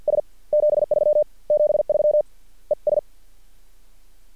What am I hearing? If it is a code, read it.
Morse code, heard as a keyed tone, about 50 wpm, spaced as for 44 wpm, 600 Hz. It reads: S 73 73 ES
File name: archived / CRSnet